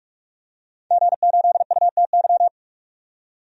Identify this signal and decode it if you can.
Morse code, heard as a keyed tone, about 44 wpm, 700 Hz. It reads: G8UTY
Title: DXCC-100 Call Signs - 5 Characters - 45wpm